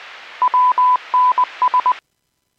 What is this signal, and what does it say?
Morse code, heard as a keyed tone, about 20 wpm, 1000 Hz. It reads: WNS
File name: Businesses / 01045 17-52 Businesses 451.25 WPS